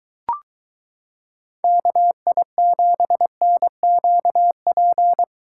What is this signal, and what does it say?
Morse code, heard as a keyed tone, about 22 wpm, 700 Hz. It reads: KI7NQP